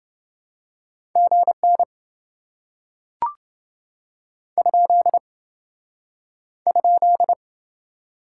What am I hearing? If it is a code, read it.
Morse code, heard as a keyed tone, about 27 wpm, 700 Hz. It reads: GN ? ?